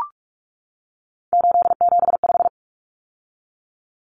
Morse code, heard as a keyed tone, about 44 wpm, 700 Hz. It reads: 875
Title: Numbers - 3-Digit 45wpm